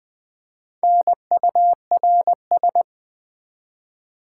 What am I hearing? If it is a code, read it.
Morse code, heard as a keyed tone, about 20 wpm, 700 Hz. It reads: NURS
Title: Build A Word - 1 Letter At A Time - 5-Letter Words - 20wpm